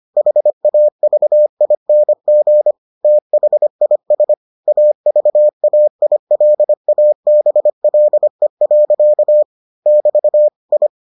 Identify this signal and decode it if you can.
Morse code, heard as a keyed tone, about 25 wpm, 600 Hz. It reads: HAVING THIS AVAILABLE. <BT> I